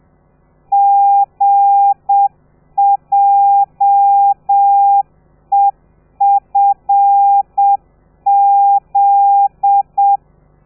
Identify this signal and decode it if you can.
Morse code, heard as a keyed tone, about 7 wpm, 800 Hz. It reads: GJEFZ
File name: telegrafia / i2tfj / Lez13 30